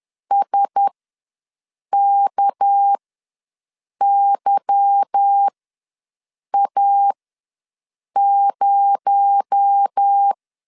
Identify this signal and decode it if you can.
Morse code, heard as a keyed tone, about 10 wpm, spaced as for 6 wpm, 800 Hz. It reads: SKYA0
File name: LEZIONI / 8 YJ45 / 8 Y ws 10wpm 3952